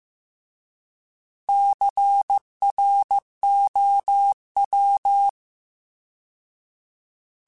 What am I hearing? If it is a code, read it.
Morse code, heard as a keyed tone, about 15 wpm, 800 Hz. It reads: CROW